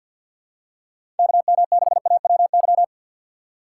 Morse code, heard as a keyed tone, about 47 wpm, 700 Hz. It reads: KM6AWY
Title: USA Call Signs - 6 Characters - 4x Character Spacing - 50wpm